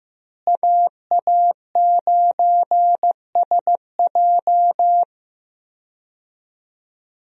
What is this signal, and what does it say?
Morse code, heard as a keyed tone, about 15 wpm, 700 Hz. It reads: AA9SJ
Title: USA Call Signs - 5 Characters - Repeated 2x - 15wpm